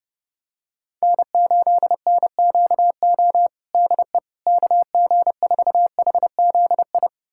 Morse code, heard as a keyed tone, about 32 wpm, 700 Hz. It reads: N8NQO DE KG4HZI
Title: USA Callsign DE Callsign 30wpm